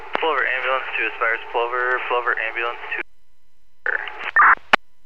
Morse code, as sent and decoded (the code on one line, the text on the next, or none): none